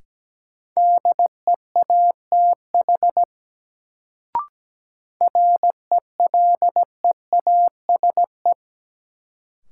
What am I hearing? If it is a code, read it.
Morse code, heard as a keyed tone, about 17 wpm, 700 Hz. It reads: DEATH RELEASE
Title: Sentences from Top 1100 Words - New Word Drill - 17wpm